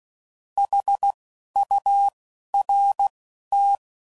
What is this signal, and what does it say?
Morse code, heard as a keyed tone, about 16 wpm, spaced as for 11 wpm, 800 Hz. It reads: HURT
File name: Prefixes / By First Letter / S vs H